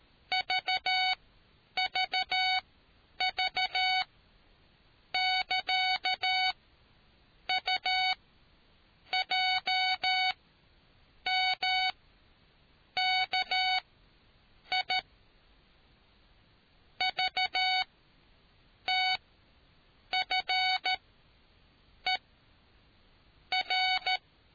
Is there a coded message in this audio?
Yes: Morse code from 0.3 to 24.2 s, heard as a keyed tone, about 13 wpm, spaced as for 7 wpm, 750 Hz. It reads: VVV <KA> UJMKI VTFER